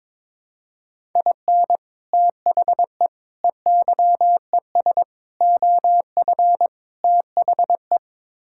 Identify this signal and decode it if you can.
Morse code, heard as a keyed tone, about 22 wpm, 700 Hz. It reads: IN THE EYES OF THE
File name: practice / 22wpm / Build A Phrase - 5-Word Phrases - 22wpm